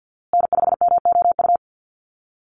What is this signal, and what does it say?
Morse code, heard as a keyed tone, about 50 wpm, 700 Hz. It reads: N5MOU
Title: USA Call Signs Repeated 3x 50wpm